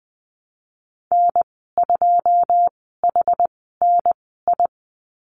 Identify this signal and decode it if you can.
Morse code, heard as a keyed tone, about 20 wpm, spaced as for 15 wpm, 700 Hz. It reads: N2HNI
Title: 122 Lesson 20g - Number 3 - Call Signs 20wpm